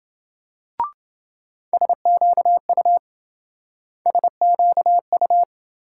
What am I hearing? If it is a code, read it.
Morse code, heard as a keyed tone, about 26 wpm, 700 Hz. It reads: SQU SQU